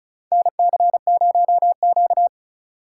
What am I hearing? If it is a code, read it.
Morse code, heard as a keyed tone, about 36 wpm, 700 Hz. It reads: NC0Q
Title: USA Call Signs - 4x Character Spacing - 35wpm